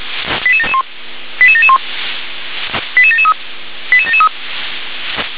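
Five-tone selective call (ZVEI), heard as quick runs of five tones, tones of about 70 ms. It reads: 88771 88771 88773 88773
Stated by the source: ZVEI five-tone selective call received with an SDR.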